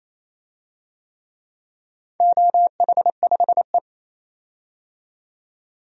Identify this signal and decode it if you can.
Morse code, heard as a keyed tone, about 28 wpm, 700 Hz. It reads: OH5E
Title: DXCC-100 Call Signs - 4 Characters - 28wpm